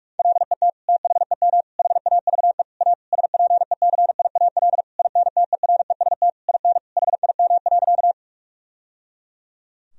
Morse code, heard as a keyed tone, about 46 wpm, 700 Hz. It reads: LET THEM HAVE A SPECIAL INTEREST IN HIM.
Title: Sentences from Top 1000 Words - Review Part 3 - 45wpm